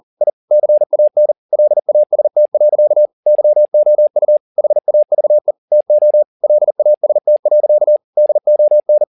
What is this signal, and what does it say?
Morse code, heard as a keyed tone, about 40 wpm, 600 Hz. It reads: I CAN LAST. YOU HAVE TO LAST. DON
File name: the old man and the sea 2 40wpm